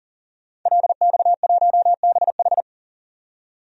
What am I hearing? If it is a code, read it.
Morse code, heard as a keyed tone, about 40 wpm, 700 Hz. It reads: LX1BH
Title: DXCC-100 Call Signs - 5 Characters - 40wpm